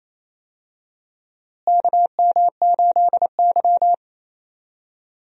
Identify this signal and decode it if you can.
Morse code, heard as a keyed tone, about 28 wpm, 700 Hz. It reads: KM8Y
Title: USA Call Signs - 4 Characters - Repeated 2x - 28wpm